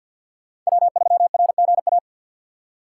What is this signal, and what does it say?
Morse code, heard as a keyed tone, about 52 wpm, 700 Hz. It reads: W3RGA
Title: K1USN SST QSO Practice - 50wpm